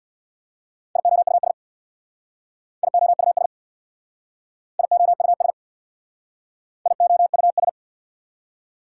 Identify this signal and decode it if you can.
Morse code, heard as a keyed tone, about 64 wpm, 690 Hz. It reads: IOUS IOUS IOUS IOUS